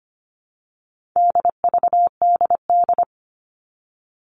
Morse code, heard as a keyed tone, about 25 wpm, 700 Hz. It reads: DVDD
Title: All Cut Numbers - 4-Digit 25wpm